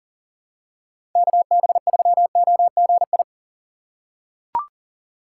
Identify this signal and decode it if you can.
Morse code, heard as a keyed tone, about 39 wpm, 700 Hz. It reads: KB3OGI
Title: USA Call Signs - 6 Characters - 3x Character Spacing - 40wpm